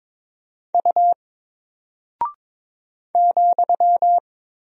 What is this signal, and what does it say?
Morse code, heard as a keyed tone, about 22 wpm, 700 Hz. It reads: U ,
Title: Single Character (Letter:Number:Punctuation:Prosign) 22wpm